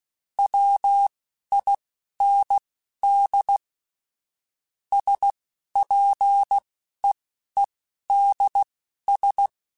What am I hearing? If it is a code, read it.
Morse code, heard as a keyed tone, about 16 wpm, spaced as for 11 wpm, 800 Hz. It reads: WIND SPEEDS
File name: Weather Sets / Weather Sets-mp3s / Weather 4 24 @10